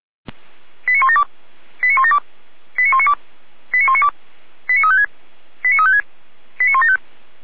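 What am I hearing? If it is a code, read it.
Five-tone selective call (ZVEI), heard as quick runs of five tones, tones of about 80 ms. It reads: 89172 89172 89192 89192 89356 89356 89175
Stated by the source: Five-tone ZVEI selective call received on an SDR.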